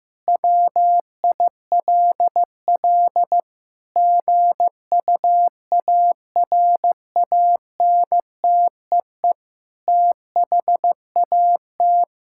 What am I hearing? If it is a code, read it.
Morse code, heard as a keyed tone, about 15 wpm, 700 Hz. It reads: WILL GUARANTEE THAT
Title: Sentences from Top 1600 Words 15wpm